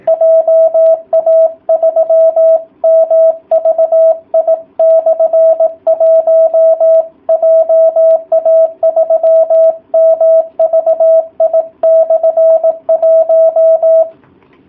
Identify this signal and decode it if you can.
Morse code, heard as a keyed tone, about 18 wpm, 660 Hz. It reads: JA3MVI/1 JA3MVI/1